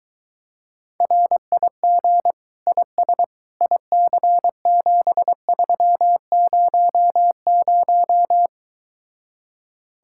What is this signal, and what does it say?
Morse code, heard as a keyed tone, about 23 wpm, 700 Hz. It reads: RIG IS IC7300